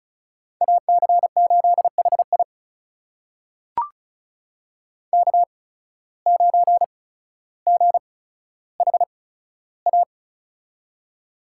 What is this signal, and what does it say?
Morse code, heard as a keyed tone, about 36 wpm, spaced as for 15 wpm, 700 Hz. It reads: AC8HI K9GHA
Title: USA Call Signs - 5 Characters - 8x Character Spacing - 35wpm